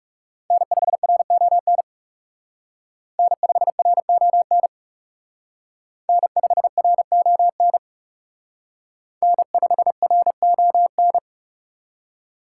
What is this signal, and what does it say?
Morse code, heard as a keyed tone, about 34 wpm, 700 Hz. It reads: N5RON N5RON N5RON N5RON
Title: Speed Racing - USA 1x3 Call Signs 30wpm